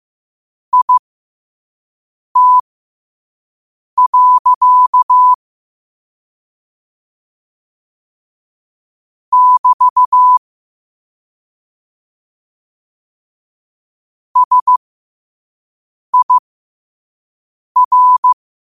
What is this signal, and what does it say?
Morse code, heard as a keyed tone, about 15 wpm, spaced as for 5 wpm, 1000 Hz. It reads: IT. <BT> SIR
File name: qotdfiles / 5wpm